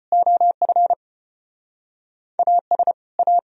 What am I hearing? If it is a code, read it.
Morse code, heard as a keyed tone, about 35 wpm, 700 Hz. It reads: OF AS A